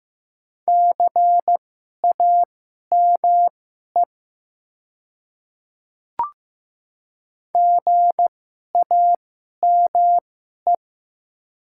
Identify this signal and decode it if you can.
Morse code, heard as a keyed tone, about 15 wpm, spaced as for 11 wpm, 700 Hz. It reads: CAME GAME
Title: 2x Character Spacing - Top 500 Words - 15wpm